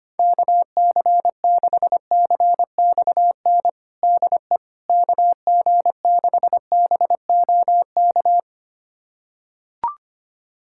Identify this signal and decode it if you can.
Morse code, heard as a keyed tone, about 25 wpm, 700 Hz. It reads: KC6CXN DE KG6BOK
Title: USA Callsign DE Callsign - 6 Characters - 25wpm